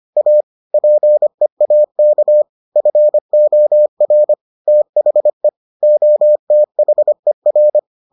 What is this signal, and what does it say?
Morse code, heard as a keyed tone, about 25 wpm, 600 Hz. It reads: A PEAK FOR THE OTHER